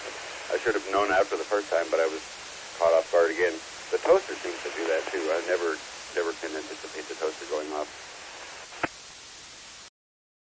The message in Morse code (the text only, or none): none